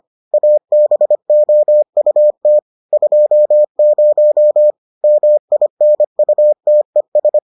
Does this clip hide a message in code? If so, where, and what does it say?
Yes: Morse code from 0.3 to 7.4 s, heard as a keyed tone, about 25 wpm, 600 Hz. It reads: ABOUT 20 MINUTES